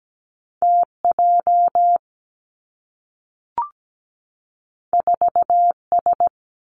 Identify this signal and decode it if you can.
Morse code, heard as a keyed tone, about 17 wpm, 700 Hz. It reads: T J 4 S
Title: Two Challenging Letter-Number Combinations 17wpm